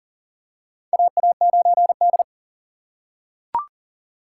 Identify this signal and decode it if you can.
Morse code, heard as a keyed tone, about 38 wpm, 700 Hz. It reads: AA9D
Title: USA Call Signs - 4 Characters - 4x Character Spacing - 40wpm